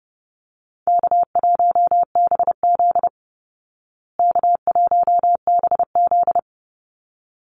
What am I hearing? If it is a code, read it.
Morse code, heard as a keyed tone, about 30 wpm, 700 Hz. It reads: K1BZ K1BZ
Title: Easy Call Signs Repeated 2x - 30wpm